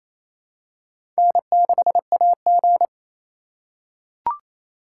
Morse code, heard as a keyed tone, about 28 wpm, 700 Hz. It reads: N6AG
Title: USA Call Signs - 4 Characters - 2x Character Spacing - 28wpm